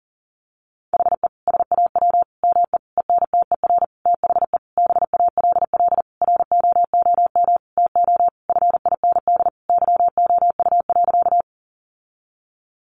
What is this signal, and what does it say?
Morse code, heard as a keyed tone, about 40 wpm, 700 Hz. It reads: HE SAW ME ENTER THE BALL ROOM TO FIND YOU.
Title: Sentences from Top 1300 Words 40wpm